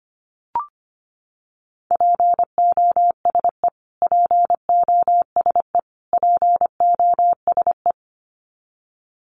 Morse code, heard as a keyed tone, about 25 wpm, 700 Hz. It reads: POSE POSE POSE
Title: Top 500 Words Repeated 25wpm